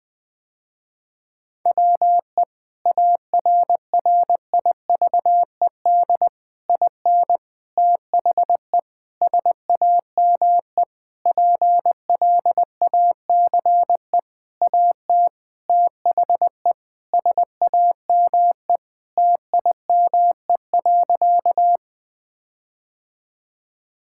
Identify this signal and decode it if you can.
Morse code, heard as a keyed tone, about 21 wpm, 700 Hz. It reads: WE ARRIVED IN THE SAME PLACE AT THE SAME TIME.
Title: Sentences with 8-Word Phrases - Spoken Beginning - 20wpm